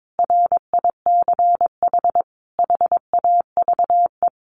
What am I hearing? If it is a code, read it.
Morse code, heard as a keyed tone, about 22 wpm, 700 Hz. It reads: RICH HAVE
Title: Sentences from Top 1100 Words 22wpm